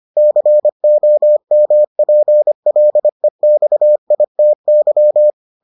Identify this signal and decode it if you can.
Morse code, heard as a keyed tone, about 25 wpm, 600 Hz. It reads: COMPLEXITY